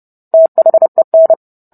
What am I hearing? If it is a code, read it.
Morse code, heard as a keyed tone, about 30 wpm, 650 Hz. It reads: THEN